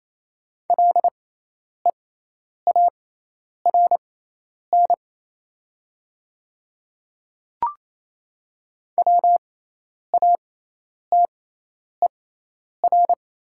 Morse code, heard as a keyed tone, about 28 wpm, spaced as for 10 wpm, 700 Hz. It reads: LEARN WATER
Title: 6x Character Spacing - Top 300 Words - 28wpm